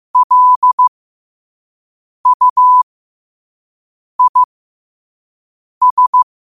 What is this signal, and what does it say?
Morse code, heard as a keyed tone, about 15 wpm, spaced as for 5 wpm, 1000 Hz. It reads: LUIS
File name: qotdfiles / 5wpm